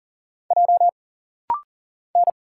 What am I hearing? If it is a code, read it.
Morse code, heard as a keyed tone, about 40 wpm, 700 Hz. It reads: J N
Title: Single Letter-Number - Rapid-Fire 40wpm